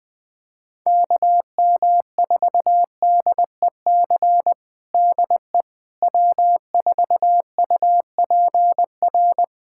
Morse code, heard as a keyed tone, about 20 wpm, 700 Hz. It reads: KM4DEC DE W4UPR
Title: USA Callsign DE Callsign 20wpm